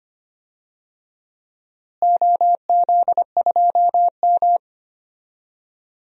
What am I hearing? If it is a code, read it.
Morse code, heard as a keyed tone, about 25 wpm, 700 Hz. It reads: OZ2M
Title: DXCC-100 Call Signs - 4 Characters - Repeated 2x - 25wpm